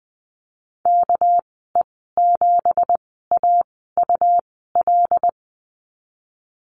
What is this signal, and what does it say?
Morse code, heard as a keyed tone, about 20 wpm, spaced as for 14 wpm, 700 Hz. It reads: KE7AUL
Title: 213 Lesson 34g - Number 8 - Call Signs 20wpm